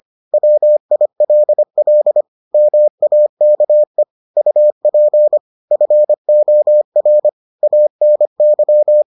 Morse code, heard as a keyed tone, about 25 wpm, 600 Hz. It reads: WILL MAKE UP FOR ANY